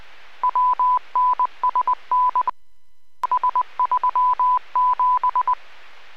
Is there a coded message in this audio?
Yes: Morse code from 0.4 to 5.5 s, heard as a keyed tone, about 20 wpm, 1000 Hz. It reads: WNSN S37